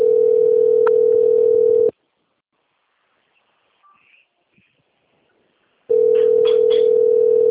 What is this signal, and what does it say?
Ringback tone (North America), 440 + 480 Hz.